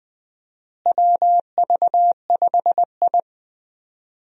Morse code, heard as a keyed tone, about 20 wpm, 700 Hz. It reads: WV5I
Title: USA Call Signs - 4 Characters - Repeated 2x - 20wpm